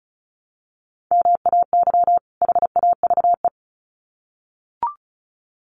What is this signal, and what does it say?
Morse code, heard as a keyed tone, about 34 wpm, 700 Hz. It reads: MAY HAVE
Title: Sets of 2 Words 35wpm